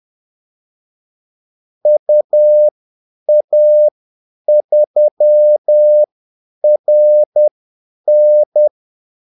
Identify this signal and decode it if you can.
Morse code, heard as a keyed tone, about 10 wpm, spaced as for 8 wpm, 600 Hz. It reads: UA3RN
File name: Callsigns-10-8wpm